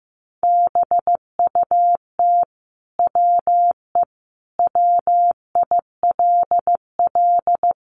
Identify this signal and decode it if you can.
Morse code, heard as a keyed tone, about 15 wpm, 700 Hz. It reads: BUT WE WILL
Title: Sentences from Top 1100 Words 15wpm